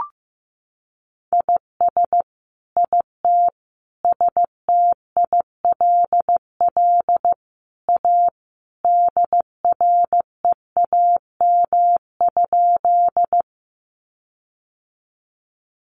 Morse code, heard as a keyed tone, about 15 wpm, 700 Hz. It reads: IS IT STILL A DREAM?
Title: Sentences from Top 1500 Words 15wpm